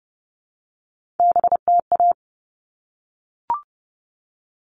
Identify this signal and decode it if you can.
Morse code, heard as a keyed tone, about 30 wpm, 700 Hz. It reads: BTA
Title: All Cut Numbers - 3-Digit 30wpm